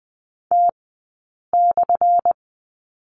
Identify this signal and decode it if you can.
Morse code, heard as a keyed tone, about 20 wpm, spaced as for 8 wpm, 700 Hz. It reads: T/
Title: 177 Lesson 29d - Slash - Two Characters 20wpm